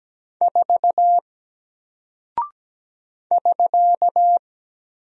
Morse code, heard as a keyed tone, about 17 wpm, 700 Hz. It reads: 4 <SK>
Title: Single Character (Letter:Number:Punctuation:Prosign) Alt 17wpm